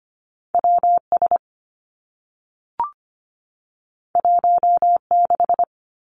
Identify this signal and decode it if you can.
Morse code, heard as a keyed tone, about 25 wpm, 700 Hz. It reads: W S 1 6
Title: Two Challenging Letter-Number Combinations 25wpm